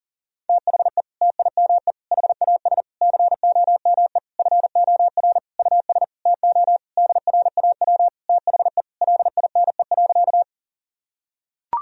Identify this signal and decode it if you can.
Morse code, heard as a keyed tone, about 39 wpm, 700 Hz. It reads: THE TIME HAS COME FOR US TO DRAW THE LINE.
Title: Sentences from Top 1000 Words - Review Part 1 - 40wpm